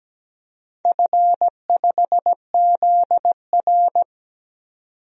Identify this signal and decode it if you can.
Morse code, heard as a keyed tone, about 17 wpm, 700 Hz. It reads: F5ZR
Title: DXCC-100 Call Signs - 4 Characters - 17wpm